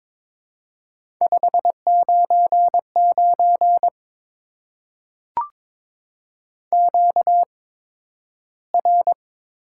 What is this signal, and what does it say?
Morse code, heard as a keyed tone, about 22 wpm, spaced as for 12 wpm, 700 Hz. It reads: 599 QR
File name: Top 90 QSO Elements - 8x Character Spacing - 22wpm